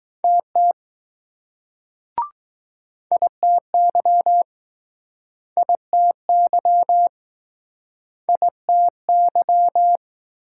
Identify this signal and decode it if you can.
Morse code, heard as a keyed tone, about 21 wpm, 700 Hz. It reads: TT ITY ITY ITY